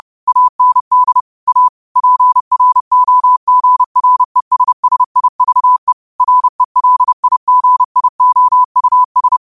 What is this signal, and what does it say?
Morse code, heard as a keyed tone, about 30 wpm, 1000 Hz. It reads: AND A PROGRESSIVE RELIGIOUS